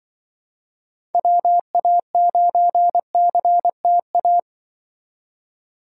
Morse code, heard as a keyed tone, about 24 wpm, 700 Hz. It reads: WA9CTA
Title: Speed Racing - USA 2x3 Call Signs 20wpm